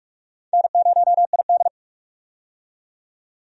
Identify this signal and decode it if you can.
Morse code, heard as a keyed tone, about 46 wpm, 700 Hz. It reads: N0ID
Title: USA Call Signs - 4 Characters - 45wpm